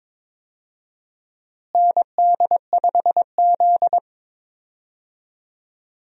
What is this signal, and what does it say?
Morse code, heard as a keyed tone, about 22 wpm, 700 Hz. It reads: ND5Z
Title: USA Call Signs - 4 Characters - Repeated 2x - 22wpm